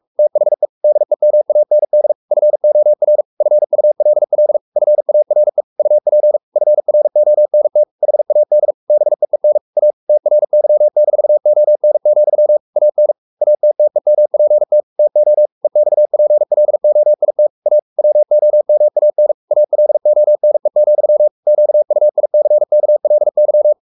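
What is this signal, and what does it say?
Morse code, heard as a keyed tone, about 44 wpm, 600 Hz. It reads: THE DEMAND FOR FULL FARE UP FRONT HAD BEEN A TRY-ON, AN ATTEMPT TO EXPLOIT A WOMAN ALONE, QUICKLY
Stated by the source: Morse code at 43 wpm sending Ken Follett's novel Never in English, from CW med Gnister.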